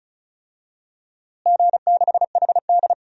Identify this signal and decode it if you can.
Morse code, heard as a keyed tone, about 34 wpm, 700 Hz. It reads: G6HD
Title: DXCC-100 Call Signs - 4 Characters - Repeated 2x - 35wpm